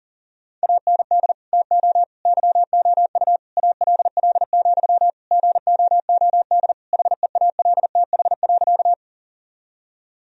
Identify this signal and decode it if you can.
Morse code, heard as a keyed tone, about 40 wpm, 700 Hz. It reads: AND TO YOU ALL, GOOD HEALTH.
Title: Sentences from Top 1000 Words - Review Part 3 - 40wpm